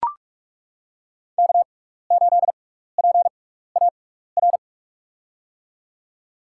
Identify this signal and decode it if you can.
Morse code, heard as a keyed tone, about 41 wpm, spaced as for 15 wpm, 700 Hz. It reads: K8PAR
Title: USA Call Signs - 5 Characters - 6x Character Spacing - 45wpm